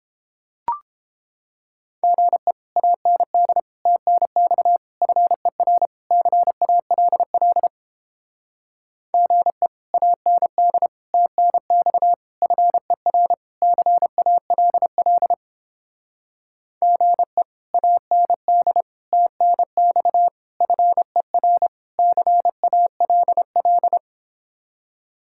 Morse code, heard as a keyed tone, about 31 wpm, 700 Hz. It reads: GE AND TNX FER CALL GE AND TNX FER CALL GE AND TNX FER CALL